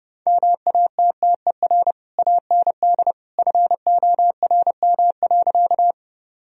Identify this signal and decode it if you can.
Morse code, heard as a keyed tone, about 32 wpm, 700 Hz. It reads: MATTER AND FORM.
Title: Sentences from Top 1000 Words - Review Part 1 - 30wpm